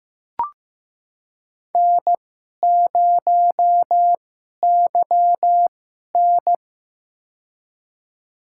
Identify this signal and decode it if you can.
Morse code, heard as a keyed tone, about 15 wpm, spaced as for 11 wpm, 700 Hz. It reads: N0YN
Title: USA Call Signs - 2x Character Spacing - 15wpm